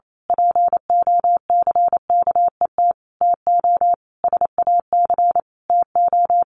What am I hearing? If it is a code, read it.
Morse code, heard as a keyed tone, about 28 wpm, 700 Hz. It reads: POCKET TO SAC TO
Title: A Sound Of Thunder 28wpm